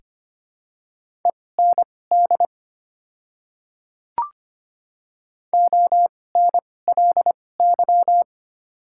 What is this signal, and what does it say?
Morse code, heard as a keyed tone, about 25 wpm, spaced as for 18 wpm, 700 Hz. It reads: END ONLY
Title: 2x Character Spacing - Top 300 Words - 25wpm